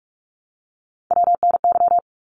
Morse code, heard as a keyed tone, about 44 wpm, 700 Hz. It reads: WNY
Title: ARRL Sections 45wpm